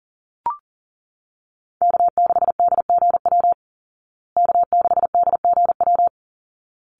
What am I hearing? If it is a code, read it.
Morse code, heard as a keyed tone, about 40 wpm, 700 Hz. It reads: K6DGW K6DGW
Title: Call Signs Repeated 3x - 40wpm